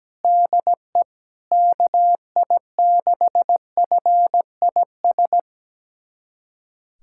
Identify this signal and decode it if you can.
Morse code, heard as a keyed tone, about 17 wpm, 700 Hz. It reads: DE KI6FIS
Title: USA Callsign DE Callsign 17wpm